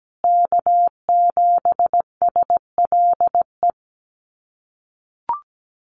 Morse code, heard as a keyed tone, about 17 wpm, 700 Hz. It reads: K7SLE
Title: USA Call Signs Repeated 3x 17wpm